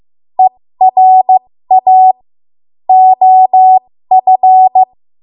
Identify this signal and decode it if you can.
Morse code, heard as a keyed tone, about 15 wpm, spaced as for 13 wpm, 750 Hz. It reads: ERA OF
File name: Morse / 13WPM